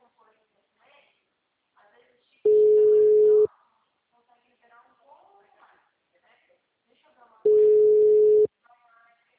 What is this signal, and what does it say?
Ringback tone (Europe, 425 Hz), with speech.